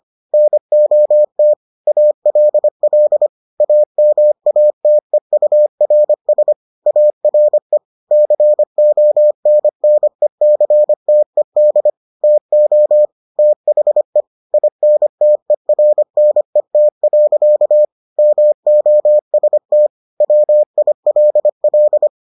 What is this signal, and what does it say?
Morse code, heard as a keyed tone, about 25 wpm, 600 Hz. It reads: NOT ALL AMATEURS ARE CONNECTED TO THE INTERNET. MOST WILL